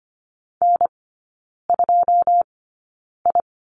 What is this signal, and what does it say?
Morse code, heard as a keyed tone, about 25 wpm, spaced as for 9 wpm, 690 Hz. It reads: N2I
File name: Lesson 15e - Number 5 - Three Characters 25wpm